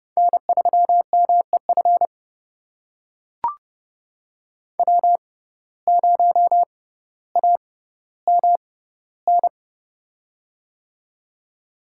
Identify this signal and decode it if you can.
Morse code, heard as a keyed tone, about 30 wpm, spaced as for 15 wpm, 700 Hz. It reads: N3MEF W0AMN
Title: USA Call Signs - 5 Characters - 6x Character Spacing - 30wpm